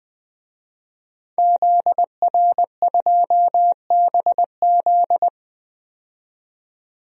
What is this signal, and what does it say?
Morse code, heard as a keyed tone, about 20 wpm, 700 Hz. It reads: ZR2BZ
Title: DXCC-100 Call Signs - 5 Characters - 20wpm